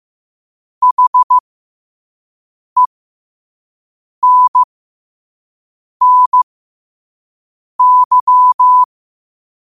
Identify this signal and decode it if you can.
Morse code, heard as a keyed tone, about 15 wpm, spaced as for 5 wpm, 1000 Hz. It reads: HENNY
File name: qotdfiles / 5wpm